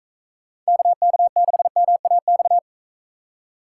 Morse code, heard as a keyed tone, about 42 wpm, 700 Hz. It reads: KK6MAX